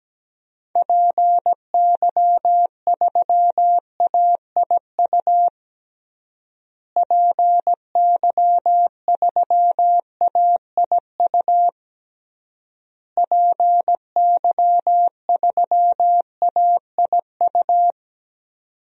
Morse code, heard as a keyed tone, about 17 wpm, 700 Hz. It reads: PY3AIU PY3AIU PY3AIU
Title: DXCC-100 Call Signs - 6 Characters - Repeated 3x - 17wpm